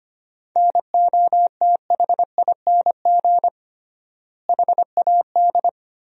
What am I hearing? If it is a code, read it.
Morse code, heard as a keyed tone, about 25 wpm, 700 Hz. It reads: NOTHING HAD
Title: 3x Word-Spacing - Sentences from Top 500 Words - 25wpm